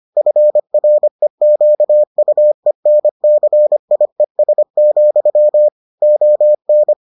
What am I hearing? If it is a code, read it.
Morse code, heard as a keyed tone, about 25 wpm, 600 Hz. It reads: FREQUENCIES, ON